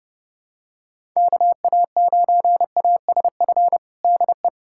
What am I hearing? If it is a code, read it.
Morse code, heard as a keyed tone, about 30 wpm, 700 Hz. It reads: KA9ASF DE
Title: USA Callsign DE Callsign 30wpm